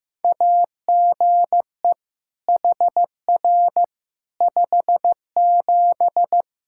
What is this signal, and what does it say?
Morse code, heard as a keyed tone, about 16 wpm, 700 Hz. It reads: AGE HR 57
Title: K1USN SST QSO Practice 15wpm